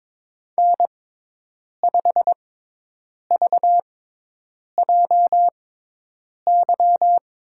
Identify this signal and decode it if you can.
Morse code, heard as a keyed tone, about 22 wpm, spaced as for 8 wpm, 700 Hz. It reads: N5VJY
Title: USA Call Signs - 5 Characters - 6x Character Spacing - 22wpm